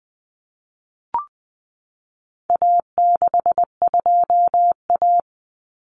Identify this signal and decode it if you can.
Morse code, heard as a keyed tone, about 20 wpm, 700 Hz. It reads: A62A
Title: Common Cut Numbers - 4-Digit 20wpm